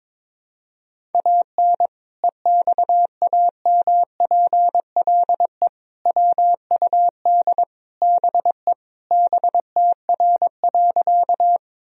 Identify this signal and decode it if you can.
Morse code, heard as a keyed tone, about 22 wpm, 700 Hz. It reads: AN EXAMPLE WUD BE BTR.
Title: Sentences from Top 400 Words and Abbreviations 22wpm